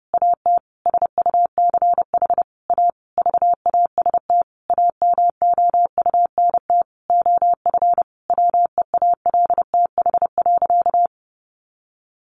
Morse code, heard as a keyed tone, about 28 wpm, 700 Hz. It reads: AT SUCH A VAST AMOUNT OF WEALTH.
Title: Sentences from Top 1300 Words 30wpm